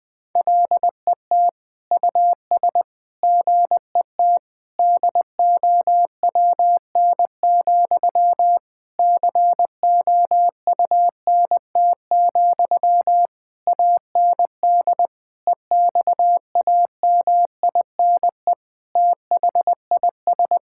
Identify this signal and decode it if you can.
Morse code, heard as a keyed tone, about 21 wpm, 700 Hz. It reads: LET US GET DOWN, COUNT, AND EXAMINE THIS
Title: Sentences from Top 1600 Words 20wpm